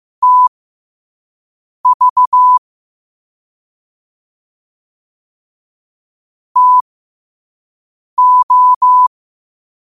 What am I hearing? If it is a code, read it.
Morse code, heard as a keyed tone, about 15 wpm, spaced as for 5 wpm, 1000 Hz. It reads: TV TO
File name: qotdfiles / 5wpm